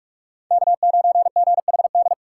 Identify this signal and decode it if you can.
Morse code, heard as a keyed tone, about 42 wpm, 690 Hz. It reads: K9GHD
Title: USA Call Signs - 8x Character Spacing - 45wpm